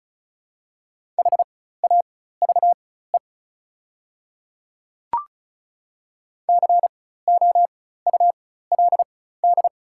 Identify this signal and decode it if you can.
Morse code, heard as a keyed tone, about 36 wpm, spaced as for 16 wpm, 700 Hz. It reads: HAVE COULD
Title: 4x Character Spacing - Top 100 Words - 35wpm